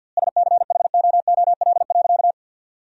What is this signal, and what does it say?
Morse code, heard as a keyed tone, about 49 wpm, 700 Hz. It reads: SCHOOL.
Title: 4x Word-Spacing - Sentences from Top 300 Words - 50wpm